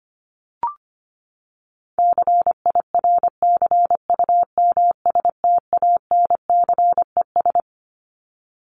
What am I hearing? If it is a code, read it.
Morse code, heard as a keyed tone, about 25 wpm, 700 Hz. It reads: CIRCUMSTANCES
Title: ES:S Suffix - Top 100 Words with Base Words 25wpm